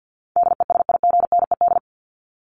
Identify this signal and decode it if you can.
Morse code, heard as a keyed tone, about 50 wpm, 700 Hz. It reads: DESIGNED
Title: ED Suffix - Top 100 Words with Base Words 50wpm